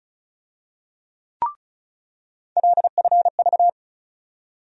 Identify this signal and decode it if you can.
Morse code, heard as a keyed tone, about 36 wpm, 700 Hz. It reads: LFV
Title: USA Call Sign Postfixes - 3 Letters 35wpm